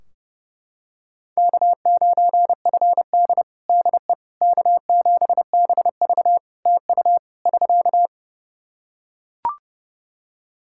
Morse code, heard as a keyed tone, about 30 wpm, 700 Hz. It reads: K9FD DE K7BV TU <SK>
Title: Practice QSO - Exchange 4 - Ending - 30wpm